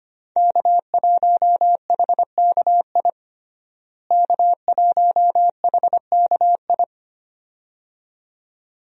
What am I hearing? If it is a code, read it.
Morse code, heard as a keyed tone, about 25 wpm, 700 Hz. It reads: K1HKI K1HKI
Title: USA Call Signs - 5 Characters - Repeated 2x - 25wpm